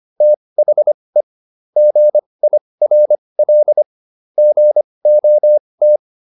Morse code, heard as a keyed tone, about 25 wpm, spaced as for 20 wpm, 600 Hz. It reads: THE GIRL GOT
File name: SS123 20